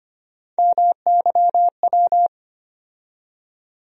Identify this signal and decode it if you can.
Morse code, heard as a keyed tone, about 25 wpm, 700 Hz. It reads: MYW
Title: USA Call Sign Postfixes - 3 Letters 25wpm